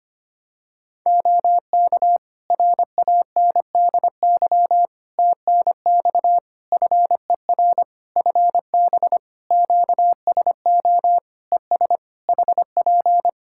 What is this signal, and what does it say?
Morse code, heard as a keyed tone, about 25 wpm, 700 Hz. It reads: OK RANDY TNX FER FB QSO ES HP
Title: Practice QSO - Exchange 4 - Ending - 25wpm